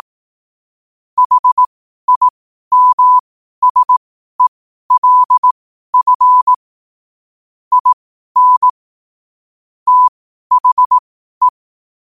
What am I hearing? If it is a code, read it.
Morse code, heard as a keyed tone, about 18 wpm, spaced as for 12 wpm, 1000 Hz. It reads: HIMSELF IN THE
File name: 12wpm